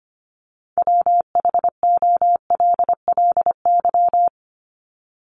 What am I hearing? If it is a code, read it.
Morse code, heard as a keyed tone, about 25 wpm, 700 Hz. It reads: WHOLLY
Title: LY Suffix - Top 100 Words with Base Words 25wpm